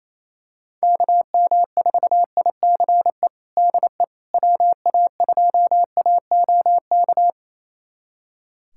Morse code, heard as a keyed tone, about 27 wpm, 700 Hz. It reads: KM4ICE DE WA2AOK
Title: USA Callsign DE Callsign - 6 Characters - 28wpm